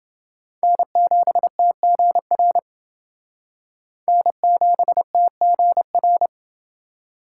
Morse code, heard as a keyed tone, about 30 wpm, 700 Hz. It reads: N7TGR N7TGR